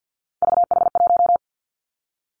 Morse code, heard as a keyed tone, about 50 wpm, 700 Hz. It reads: VH1
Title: Three Challenging Letter-Number Combinations 50wpm